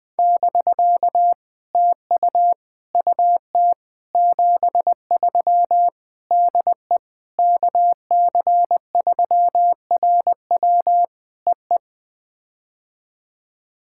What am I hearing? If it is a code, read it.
Morse code, heard as a keyed tone, about 18 wpm, 700 Hz. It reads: <BK> TU UT 73 DE KC3RW EE
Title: POTA QSO Practice - 20wpm